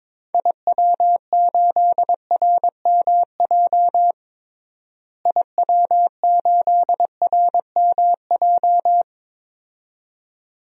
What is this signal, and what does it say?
Morse code, heard as a keyed tone, about 22 wpm, 700 Hz. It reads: IW8RMJ IW8RMJ
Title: DXCC-100 Call Signs - 6 Characters - Repeated 2x - 22wpm